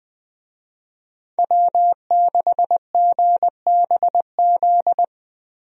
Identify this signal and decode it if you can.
Morse code, heard as a keyed tone, about 20 wpm, 700 Hz. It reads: W6GBZ
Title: USA Call Signs - 5 Characters - Repeated 2x - 20wpm